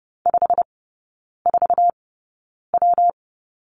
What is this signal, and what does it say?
Morse code, heard as a keyed tone, about 30 wpm, spaced as for 9 wpm, 700 Hz. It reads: 54W
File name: Lesson 19e - Letter w - Three Characters 30wpm